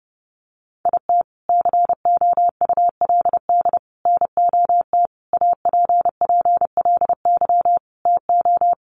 Morse code, heard as a keyed tone, about 30 wpm, 700 Hz. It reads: IT COULD NOT APPLY TO
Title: Sentences from Top 1200 Words 30wpm